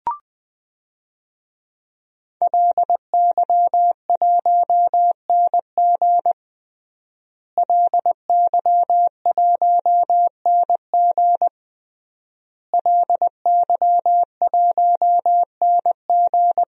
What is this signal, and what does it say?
Morse code, heard as a keyed tone, about 20 wpm, 700 Hz. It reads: LY1NG LY1NG LY1NG
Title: DXCC-100 Call Signs - 5 Characters - Repeated 3x - 20wpm